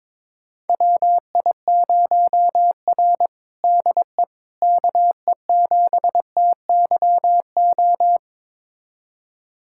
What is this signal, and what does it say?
Morse code, heard as a keyed tone, about 22 wpm, 700 Hz. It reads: WI0R DE KE7TYO
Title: USA Callsign DE Callsign 22wpm